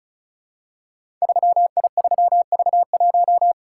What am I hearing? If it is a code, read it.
Morse code, heard as a keyed tone, about 36 wpm, 700 Hz. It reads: 3I3V1
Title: Code Groups - 5 Letter:Numbers - 35wpm